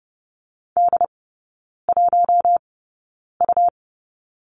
Morse code, heard as a keyed tone, about 30 wpm, spaced as for 9 wpm, 700 Hz. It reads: D1U
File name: Lesson 17e - Letter u - Three Characters 30wpm